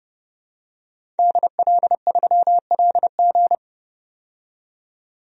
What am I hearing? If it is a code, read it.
Morse code, heard as a keyed tone, about 29 wpm, 700 Hz. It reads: DL3LG
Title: DXCC-100 Call Signs - 5 Characters - 30wpm